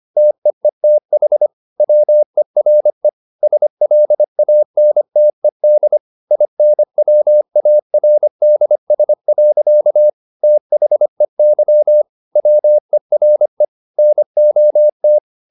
Morse code, heard as a keyed tone, about 25 wpm, 600 Hz. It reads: TEETH WERE SLANTED INWARDS. THEY WERE NOT